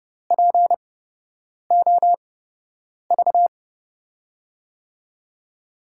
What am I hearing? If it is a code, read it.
Morse code, heard as a keyed tone, about 30 wpm, spaced as for 8 wpm, 700 Hz. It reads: POV
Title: USA Call Signs - 6 Characters - 8x Character Spacing - 30wpm